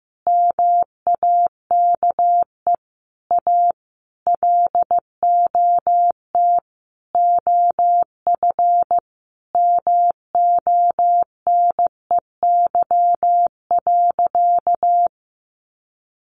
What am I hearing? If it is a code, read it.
Morse code, heard as a keyed tone, about 15 wpm, 700 Hz. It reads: MAKE A LOT OF MONEY.
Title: Easy Phrases 15wpm